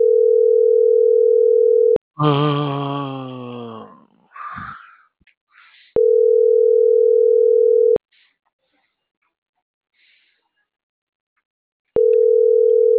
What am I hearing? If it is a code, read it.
Ringback tone (North America), 440 + 480 Hz.